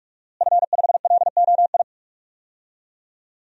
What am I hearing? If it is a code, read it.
Morse code, heard as a keyed tone, about 43 wpm, 700 Hz. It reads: F5LOI
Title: DXCC-100 Call Signs - 5 Characters - Repeated 3x - 45wpm